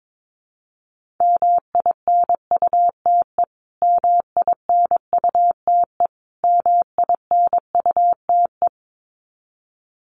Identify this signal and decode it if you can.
Morse code, heard as a keyed tone, about 22 wpm, 700 Hz. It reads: MINUTE MINUTE MINUTE
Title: Top 500 Words Repeated 22wpm